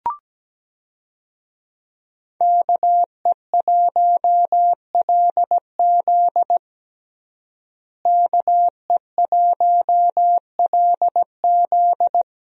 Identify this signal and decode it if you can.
Morse code, heard as a keyed tone, about 17 wpm, 700 Hz. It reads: KE1LZ KE1LZ
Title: USA Call Signs - 5 Characters - Repeated 2x - 17wpm